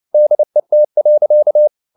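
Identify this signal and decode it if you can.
Morse code, heard as a keyed tone, about 29 wpm, 600 Hz. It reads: DET.
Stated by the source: Fast (QRQ) Morse at 29 wpm, from CW med Gnister.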